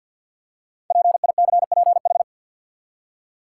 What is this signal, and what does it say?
Morse code, heard as a keyed tone, about 48 wpm, 700 Hz. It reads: PICPH
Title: Code Groups - 5 Letters - 50wpm